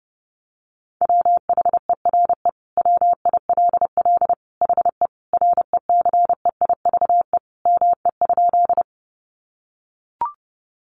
Morse code, heard as a keyed tone, about 30 wpm, 700 Hz. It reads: WHERE WILL HE RECEIVE ME?
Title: Sentences from Top 1000 Words 30wpm